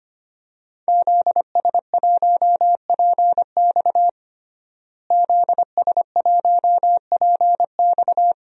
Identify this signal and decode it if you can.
Morse code, heard as a keyed tone, about 25 wpm, 700 Hz. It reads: ZS1PX ZS1PX
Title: DXCC-100 Call Signs - 5 Characters - Repeated 2x - 25wpm